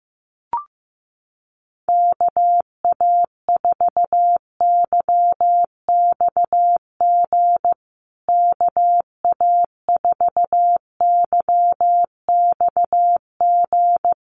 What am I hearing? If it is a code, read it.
Morse code, heard as a keyed tone, about 15 wpm, 700 Hz. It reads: KA4YXG KA4YXG
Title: USA Call Signs Repeated 2x 15wpm